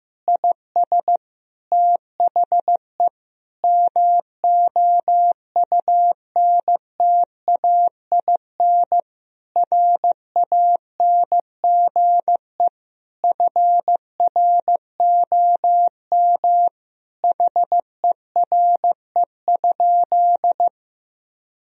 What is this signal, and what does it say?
Morse code, heard as a keyed tone, about 15 wpm, 700 Hz. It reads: IS THE MOUNTAIN RANGE FROM HERE?
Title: Sentences from Top 1000 Words - Review Part 4 - 15wpm